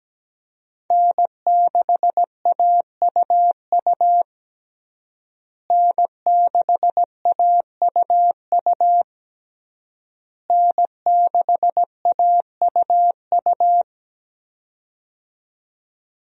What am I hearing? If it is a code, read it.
Morse code, heard as a keyed tone, about 17 wpm, 700 Hz. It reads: N6AUU N6AUU N6AUU
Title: USA Call Signs - 5 Characters - Repeated 3x - 17wpm